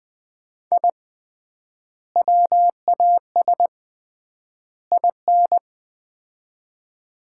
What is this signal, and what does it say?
Morse code, heard as a keyed tone, about 20 wpm, 700 Hz. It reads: I WAS IN
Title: 3x Word-Spacing - Sets of 3 Words - 20wpm